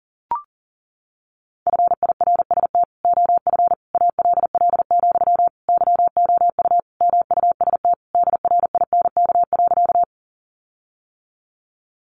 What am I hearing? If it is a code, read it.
Morse code, heard as a keyed tone, about 40 wpm, 700 Hz. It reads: FIRST OF ALL, YOU MUST DRINK.
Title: Sentences from Top 1500 Words 40wpm